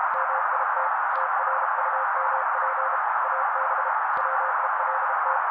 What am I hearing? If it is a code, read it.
Morse code, heard as a keyed tone, about 32 wpm, 530 Hz. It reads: MIT TRUMP AB WELT
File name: daily / 31wpm noise